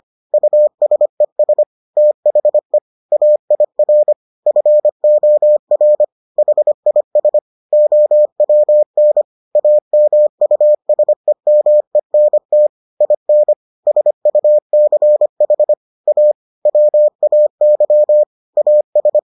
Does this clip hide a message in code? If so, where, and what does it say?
Yes: Morse code from 0.3 to 19.2 s, heard as a keyed tone, about 25 wpm, 600 Hz. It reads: USES THE AIR FOR HIS OWN AMUSEMENT IN SUCH A WAY AS